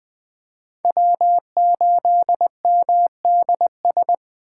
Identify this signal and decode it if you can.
Morse code, heard as a keyed tone, about 20 wpm, 700 Hz. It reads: W8MDS
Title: USA Call Signs - 5 Characters - 2x Character Spacing - 20wpm